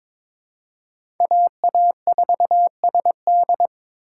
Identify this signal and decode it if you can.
Morse code, heard as a keyed tone, about 22 wpm, 700 Hz. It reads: AA4SD
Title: USA Call Signs - 5 Characters - 4x Character Spacing - 22wpm